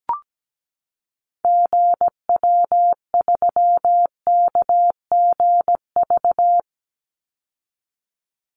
Morse code, heard as a keyed tone, about 17 wpm, 700 Hz. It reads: GW3KGV
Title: Call Signs Repeated 3x - 17wpm